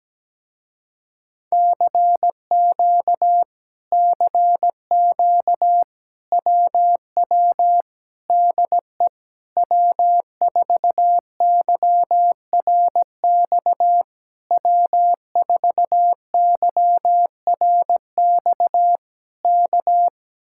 Morse code, heard as a keyed tone, about 17 wpm, 700 Hz. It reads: CQ CQ WW DE W4YRX W4YRX K
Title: Speed-Racing Common QSO Phrases - 17wpm